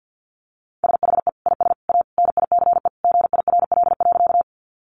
Morse code, heard as a keyed tone, about 50 wpm, 700 Hz. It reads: SHE IS A NICE GIRL.
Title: Sentences from Top 1100 Words 50wpm